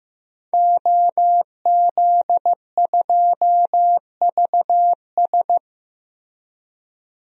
Morse code, heard as a keyed tone, about 15 wpm, 700 Hz. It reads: OZ2VS
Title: DXCC-100 Call Signs - 5 Characters - Repeated 2x - 15wpm